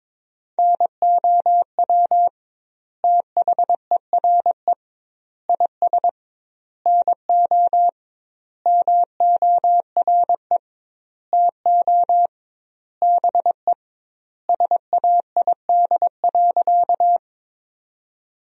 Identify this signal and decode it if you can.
Morse code, heard as a keyed tone, about 22 wpm, 700 Hz. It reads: NOW THERE IS NO MORE TO BE SAID.
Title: 2x Word-Spacing - Sentences from Top 100 Words - 22wpm